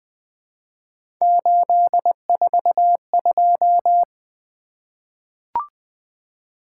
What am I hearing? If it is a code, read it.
Morse code, heard as a keyed tone, about 20 wpm, 700 Hz. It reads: 842
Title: Three Letter-Number Combinations - 20wpm